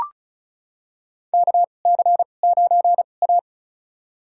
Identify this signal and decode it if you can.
Morse code, heard as a keyed tone, about 34 wpm, spaced as for 25 wpm, 700 Hz. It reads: KC9A
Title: USA Call Signs - 4 Characters - 2x Character Spacing - 35wpm